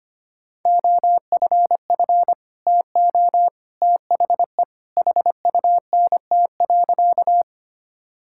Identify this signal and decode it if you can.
Morse code, heard as a keyed tone, about 25 wpm, 700 Hz. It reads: OFF TO THE HUNT.
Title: Sentences from Top 1600 Words 25wpm